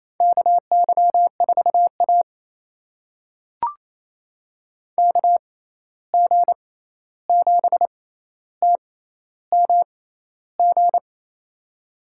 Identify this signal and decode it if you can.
Morse code, heard as a keyed tone, about 28 wpm, spaced as for 13 wpm, 700 Hz. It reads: KY4A KG7TMG
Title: USA Call Signs - 6x Character Spacing - 28wpm